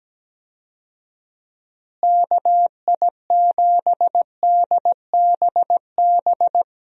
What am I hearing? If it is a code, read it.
Morse code, heard as a keyed tone, about 17 wpm, 700 Hz. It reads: KI7DBB
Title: USA Call Signs - 6 Characters - Repeated 2x - 17wpm